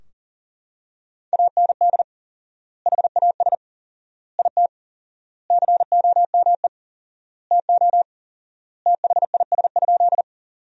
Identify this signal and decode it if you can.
Morse code, heard as a keyed tone, about 40 wpm, 700 Hz. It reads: AND HAS IT COME TO THIS?
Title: 4x Word-Spacing - Sentences from Top 100 Words - 40wpm